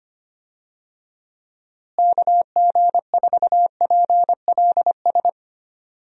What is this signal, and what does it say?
Morse code, heard as a keyed tone, about 25 wpm, 700 Hz. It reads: KG4PLS
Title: USA Call Signs - 6 Characters - Repeated 3x - 25wpm